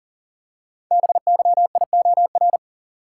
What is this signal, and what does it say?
Morse code, heard as a keyed tone, about 38 wpm, 690 Hz. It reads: BYIOR
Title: Code Groups - 5 Letters - 40wpm